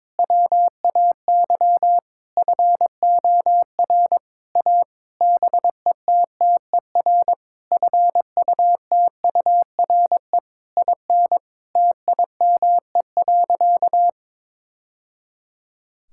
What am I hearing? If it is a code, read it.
Morse code, heard as a keyed tone, about 22 wpm, 700 Hz. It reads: WAY FOR A BETTER FUTURE IN TIME.
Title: Sentences from Top 1000 Words - Review Part 6 - 22wpm